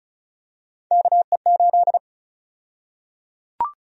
Morse code, heard as a keyed tone, about 34 wpm, 700 Hz. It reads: KE8
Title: USA Call Sign Prefixes - Repeated 2x - 35wpm